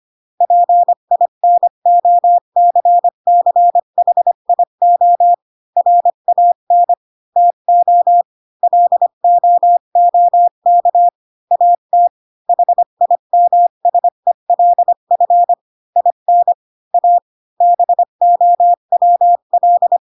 Morse code, heard as a keyed tone, about 25 wpm, 700 Hz. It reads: PINOCCHIO RAN TO LOOK AT HIMSELF IN A BOWL
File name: Pinocchio / Pinocchio ch8 20 25 ik1hge